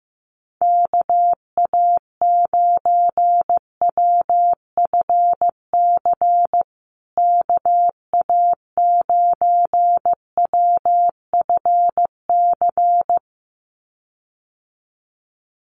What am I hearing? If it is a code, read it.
Morse code, heard as a keyed tone, about 15 wpm, 700 Hz. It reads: KA9WFC KA9WFC
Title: USA Call Signs Repeated 2x 15wpm